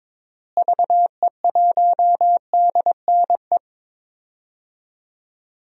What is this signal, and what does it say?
Morse code, heard as a keyed tone, about 21 wpm, 700 Hz. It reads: VE1DNE
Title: DXCC-100 Call Signs - 6 Characters - Repeated 2x - 22wpm